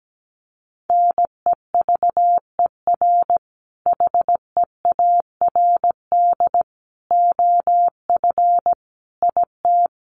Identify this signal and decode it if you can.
Morse code, heard as a keyed tone, about 17 wpm, 700 Hz. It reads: NEVER HEARD OF IT
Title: Sentences from Top 400 Words 17wpm